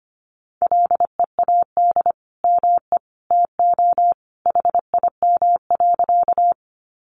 Morse code, heard as a keyed tone, about 25 wpm, 700 Hz. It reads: LEAD ME TO HIM.
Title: Sentences from Top 500 Words 25wpm